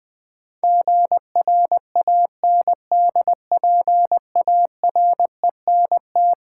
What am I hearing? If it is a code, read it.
Morse code, heard as a keyed tone, about 20 wpm, 700 Hz. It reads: GRANDPARENT
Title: Build A Word - 4 Letters At A Time - 11-Letter Words - 20wpm